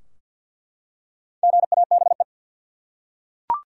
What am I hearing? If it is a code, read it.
Morse code, heard as a keyed tone, about 49 wpm, 700 Hz. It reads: GABE
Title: USA Common Names - Max 5 Letters - 50wpm